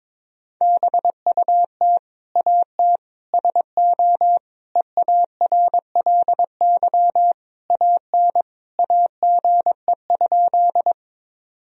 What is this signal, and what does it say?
Morse code, heard as a keyed tone, about 22 wpm, 700 Hz. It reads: BUT AT SO EARLY AN AGE?
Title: Sentences from Top 1000 Words - Review Part 3 - 22wpm